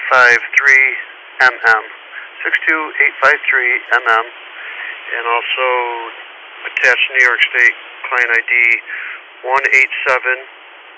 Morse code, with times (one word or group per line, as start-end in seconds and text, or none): none